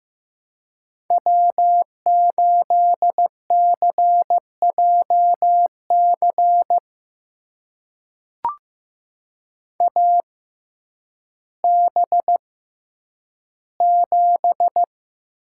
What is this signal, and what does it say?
Morse code, heard as a keyed tone, about 15 wpm, spaced as for 9 wpm, 700 Hz. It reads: W8CJC AB7
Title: USA Call Signs - 5 Characters - 6x Character Spacing - 15wpm